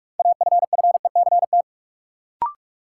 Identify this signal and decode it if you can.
Morse code, heard as a keyed tone, about 46 wpm, 700 Hz. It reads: AFFECT
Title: Sentences from Top 900 Words - New Word Drill - 45wpm